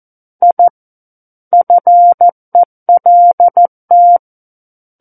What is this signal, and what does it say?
Morse code, heard as a keyed tone, about 14 wpm, 700 Hz. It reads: I FELT
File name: Book One / 05-Chapter 5 - 14 WPM